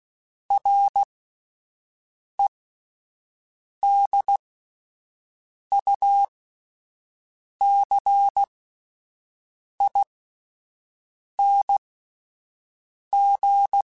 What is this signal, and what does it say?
Morse code, heard as a keyed tone, about 16 wpm, spaced as for 5 wpm, 800 Hz. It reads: REDUCING
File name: (spaced) / QSOs Spaced-mp3s / WD0FBT 1 @10 sp